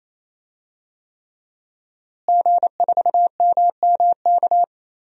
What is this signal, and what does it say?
Morse code, heard as a keyed tone, about 28 wpm, 700 Hz. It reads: G4MMK
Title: DXCC-100 Call Signs - 5 Characters - Repeated 3x - 28wpm